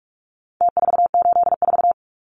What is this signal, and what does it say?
Morse code, heard as a keyed tone, about 44 wpm, 700 Hz. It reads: T484
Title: Numbers - Contest Style - 4-Digit 45wpm